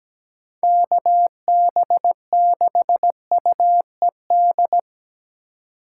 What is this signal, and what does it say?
Morse code, heard as a keyed tone, about 20 wpm, 700 Hz. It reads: KB6UED
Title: Speed Racing - USA 2x3 Call Signs 17wpm